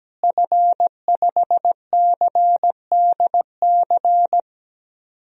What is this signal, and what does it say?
Morse code, heard as a keyed tone, about 17 wpm, 700 Hz. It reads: F5CDC
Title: DXCC-100 Call Signs - 5 Characters - Repeated 2x - 17wpm